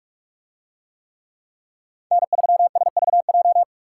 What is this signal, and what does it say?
Morse code, heard as a keyed tone, about 44 wpm, 700 Hz. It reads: N3SVJ